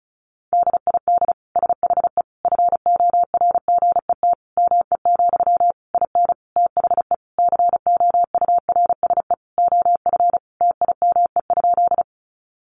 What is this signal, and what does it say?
Morse code, heard as a keyed tone, about 34 wpm, 700 Hz. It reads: DID SHE FORGET ME, IN THE COURSE OF TIME?
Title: Sentences from Top 1200 Words 35wpm